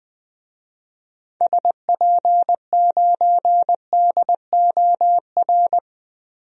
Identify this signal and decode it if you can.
Morse code, heard as a keyed tone, about 21 wpm, 700 Hz. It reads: SP9DOR
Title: DXCC-100 Call Signs - 6 Characters - Repeated 3x - 20wpm